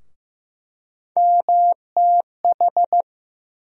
Morse code, heard as a keyed tone, about 15 wpm, 700 Hz. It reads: MTH
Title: USA Call Sign Postfixes - 3 Letters 15wpm